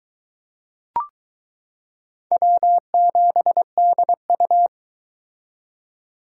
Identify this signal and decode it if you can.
Morse code, heard as a keyed tone, about 21 wpm, 700 Hz. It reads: W7DU